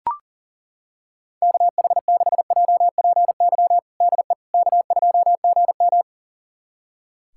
Morse code, heard as a keyed tone, about 40 wpm, 700 Hz. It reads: KH6JPY DE K2GM
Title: USA Callsign DE Callsign 40wpm